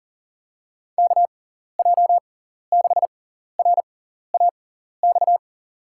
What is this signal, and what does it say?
Morse code, heard as a keyed tone, about 37 wpm, spaced as for 14 wpm, 700 Hz. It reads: KJ6RAX
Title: USA Call Signs - 6 Characters - 6x Character Spacing - 40wpm